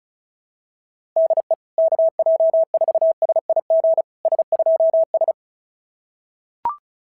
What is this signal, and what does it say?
Morse code, heard as a keyed tone, about 36 wpm, 650 Hz. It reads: DE KJ4SIG S2S
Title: SOTA QSO Practice 35wpm